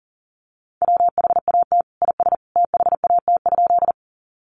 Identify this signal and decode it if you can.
Morse code, heard as a keyed tone, about 40 wpm, 700 Hz. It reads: WHAT IS THAT?
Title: Sentences from Top 100 Words 40wpm